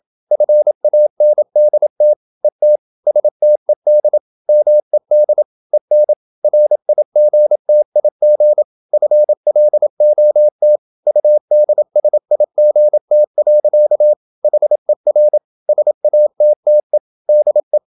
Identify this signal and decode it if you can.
Morse code, heard as a keyed tone, about 28 wpm, 600 Hz. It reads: FANDT ET STED MED EN RIGTIG FLOT UDSIGT. HER SATTE DE
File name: Ferie QRQ 27wpm